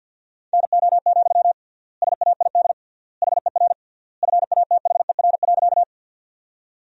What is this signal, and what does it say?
Morse code, heard as a keyed tone, about 50 wpm, 700 Hz. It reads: NO, SAID HER FATHER.
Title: 3x Word-Spacing - Sentences from Top 300 Words - 50wpm